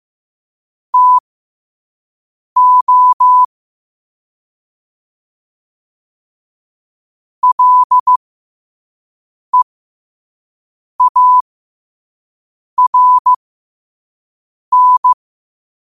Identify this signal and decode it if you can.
Morse code, heard as a keyed tone, about 15 wpm, spaced as for 5 wpm, 1000 Hz. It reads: TO LEARN
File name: qotdfiles / 5wpm